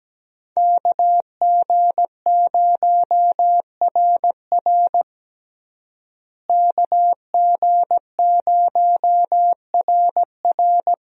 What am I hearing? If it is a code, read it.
Morse code, heard as a keyed tone, about 17 wpm, 700 Hz. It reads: KG0RR KG0RR
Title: USA Call Signs - 5 Characters - Repeated 3x - 17wpm